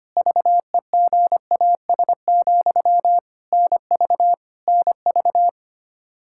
Morse code, heard as a keyed tone, about 25 wpm, 700 Hz. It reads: VEGAS, NV NV
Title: K1USN SST QSO Practice 25wpm